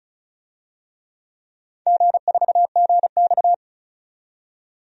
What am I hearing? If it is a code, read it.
Morse code, heard as a keyed tone, about 36 wpm, 690 Hz. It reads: G4GX
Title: DXCC-100 Call Signs - 4 Characters - 35wpm